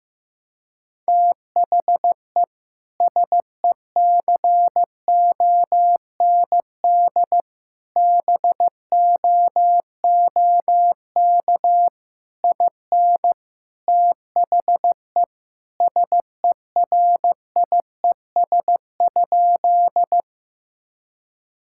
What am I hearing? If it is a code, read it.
Morse code, heard as a keyed tone, about 15 wpm, 700 Hz. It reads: THE SECOND BOOK IN THE SERIES?
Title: Sentences from Top 1000 Words - Review Part 5 - 15wpm